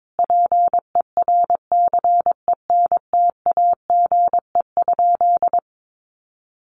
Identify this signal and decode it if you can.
Morse code, heard as a keyed tone, about 22 wpm, 700 Hz. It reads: PERCENTAGE?
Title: Sentences from Top 1300 Words 22wpm